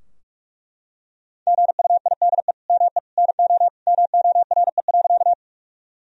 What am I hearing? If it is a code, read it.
Morse code, heard as a keyed tone, about 41 wpm, 700 Hz. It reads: GUIDE ME NO MORE.
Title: Sentences from Top 1000 Words - Review Part 3 - 45wpm